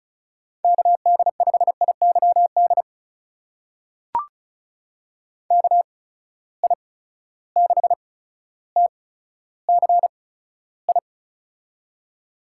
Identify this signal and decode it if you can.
Morse code, heard as a keyed tone, about 36 wpm, spaced as for 15 wpm, 700 Hz. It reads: KD5IYD KI6TCI
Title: USA Call Signs - 6 Characters - 8x Character Spacing - 35wpm